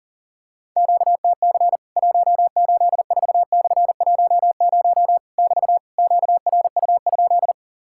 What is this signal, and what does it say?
Morse code, heard as a keyed tone, about 40 wpm, 700 Hz. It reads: QTC 184/10 <BT> QRU?
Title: Worked All Europe DX Contest - QTC Exchange - 40wpm